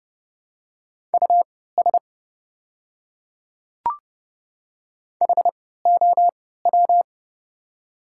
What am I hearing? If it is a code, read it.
Morse code, heard as a keyed tone, about 32 wpm, spaced as for 17 wpm, 700 Hz. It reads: US HOW
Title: 3x Character Spacing - Top 200 Words - 30wpm